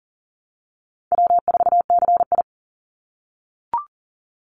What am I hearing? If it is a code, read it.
Morse code, heard as a keyed tone, about 40 wpm, 700 Hz. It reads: W4CI
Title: Easy Call Signs - 40wpm